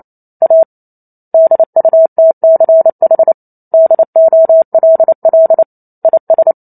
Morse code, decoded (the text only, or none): A DUTCH DOLL IS